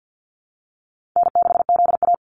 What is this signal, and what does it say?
Morse code, heard as a keyed tone, about 50 wpm, 700 Hz. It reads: N6ZA
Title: NA SOTA Activator Callsigns 50wpm